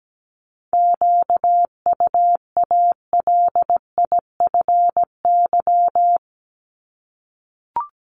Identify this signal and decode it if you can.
Morse code, heard as a keyed tone, about 17 wpm, 700 Hz. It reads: QUALIFY
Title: Top 1751-2000 Words 17wpm